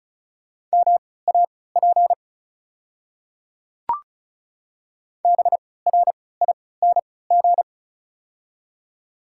Morse code, heard as a keyed tone, about 34 wpm, spaced as for 20 wpm, 690 Hz. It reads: MAP BRING
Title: 3x Character Spacing - Top 500 Words - 35wpm